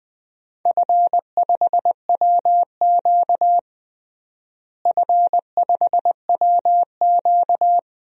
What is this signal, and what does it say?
Morse code, heard as a keyed tone, about 20 wpm, 700 Hz. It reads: F5WQ F5WQ
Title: DXCC-100 Call Signs - 4 Characters - Repeated 2x - 20wpm